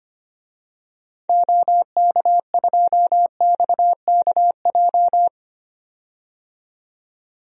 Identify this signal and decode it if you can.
Morse code, heard as a keyed tone, about 25 wpm, 700 Hz. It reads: OK2XKJ
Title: DXCC-100 Call Signs - 6 Characters - 25wpm